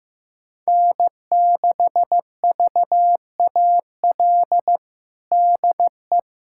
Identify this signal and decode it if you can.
Morse code, heard as a keyed tone, about 15 wpm, 700 Hz. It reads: N6VAL DE
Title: USA Callsign DE Callsign - 5 Characters - 15wpm